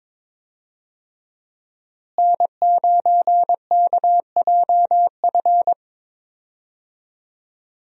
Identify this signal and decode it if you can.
Morse code, heard as a keyed tone, about 22 wpm, 700 Hz. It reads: N9KJF
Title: USA Call Signs - 5 Characters - Repeated 3x - 22wpm